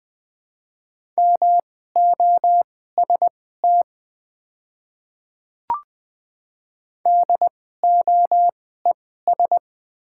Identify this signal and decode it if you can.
Morse code, heard as a keyed tone, about 21 wpm, spaced as for 15 wpm, 700 Hz. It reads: MOST DOES
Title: 2x Character Spacing - Top 200 Words - 20wpm